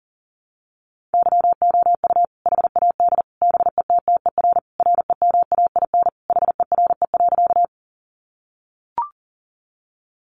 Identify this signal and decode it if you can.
Morse code, heard as a keyed tone, about 40 wpm, 700 Hz. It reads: YOU HAD BETTER REMAIN HERE.
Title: Sentences from Top 1200 Words 40wpm